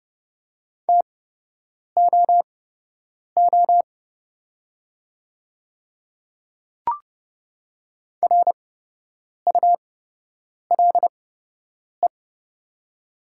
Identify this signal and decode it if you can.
Morse code, heard as a keyed tone, about 30 wpm, spaced as for 8 wpm, 690 Hz. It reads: TOO RULE
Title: 8x Character Spacing - Top 500 Words - 30wpm